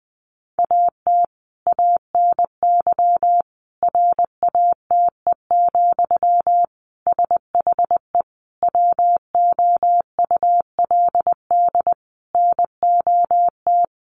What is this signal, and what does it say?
Morse code, heard as a keyed tone, about 20 wpm, 700 Hz. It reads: AT ANY RATE, SHE WOULD NOT
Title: Sentences from Top 1300 Words 20wpm